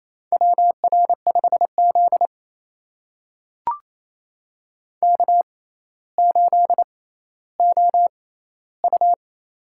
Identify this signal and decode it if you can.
Morse code, heard as a keyed tone, about 28 wpm, spaced as for 14 wpm, 700 Hz. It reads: WR5Z K8OU
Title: USA Call Signs - 4 Characters - 6x Character Spacing - 28wpm